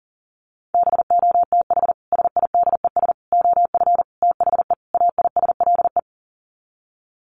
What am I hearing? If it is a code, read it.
Morse code, heard as a keyed tone, about 40 wpm, 700 Hz. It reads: BOTH SIDES OF THE AISLE
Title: Sets of 5 Words Encore2 40wpm